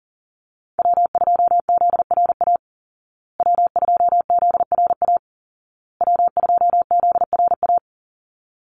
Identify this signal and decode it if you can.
Morse code, heard as a keyed tone, about 40 wpm, 700 Hz. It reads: W2ZRA W2ZRA W2ZRA
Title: Call Signs Repeated 2x - 40wpm